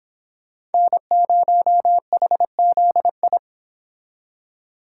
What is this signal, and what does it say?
Morse code, heard as a keyed tone, about 31 wpm, 700 Hz. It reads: N0HZI